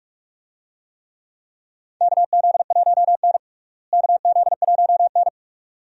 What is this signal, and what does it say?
Morse code, heard as a keyed tone, about 47 wpm, 700 Hz. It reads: KZ1N KZ1N
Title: USA Call Signs - 4 Characters - Repeated 2x - 45wpm